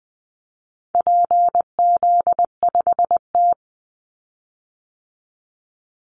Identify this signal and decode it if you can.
Morse code, heard as a keyed tone, about 20 wpm, 700 Hz. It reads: PZ5T
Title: Call Signs Repeated 2x - 20wpm